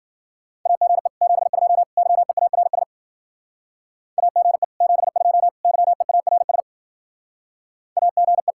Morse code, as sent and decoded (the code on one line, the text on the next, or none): .- --. / --... ..--- / -.-- . .- .-. ... / .- --. / --... ..--- / -.-- . .- .-. ... / .- --. .
AGE 72 YEARS AGE 72 YEARS AGE